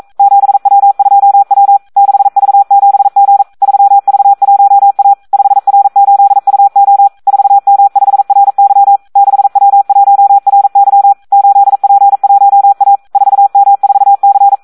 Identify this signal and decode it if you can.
Morse code, heard as a keyed tone, about 42 wpm, 800 Hz. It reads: 7W2W 6V7G 3V1A 5R8UO 4M5RY 6W1RY 8P1A 4M4C
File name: Audio practica CW / SD 1 42